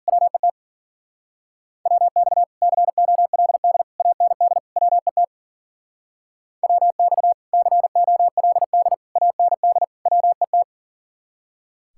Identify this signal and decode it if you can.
Morse code, heard as a keyed tone, about 50 wpm, 700 Hz. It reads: WET WX COLD AND WET WX COLD AND WET
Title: Speed-Racing Common QSO Phrases - 40wpm